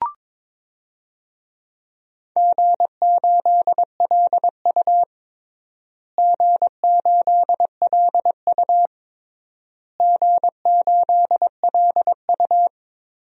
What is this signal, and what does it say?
Morse code, heard as a keyed tone, about 22 wpm, 700 Hz. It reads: G8LU G8LU G8LU
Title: DXCC-100 Call Signs - 4 Characters - Repeated 3x - 22wpm